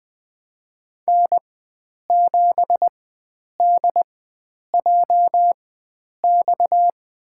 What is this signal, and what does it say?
Morse code, heard as a keyed tone, about 20 wpm, spaced as for 9 wpm, 700 Hz. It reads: N7DJX
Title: USA Call Signs - 5 Characters - 4x Character Spacing - 20wpm